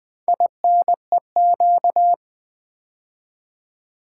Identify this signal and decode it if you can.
Morse code, heard as a keyed tone, about 20 wpm, 700 Hz. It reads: INEQ
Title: Build A Word - 4 Letters At A Time - 10-Letter Words - 20wpm